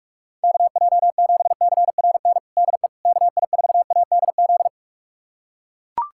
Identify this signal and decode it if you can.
Morse code, heard as a keyed tone, about 41 wpm, 700 Hz. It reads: KJ7CRN DE KI4ADZ
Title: USA Callsign DE Callsign - 6 Characters - 45wpm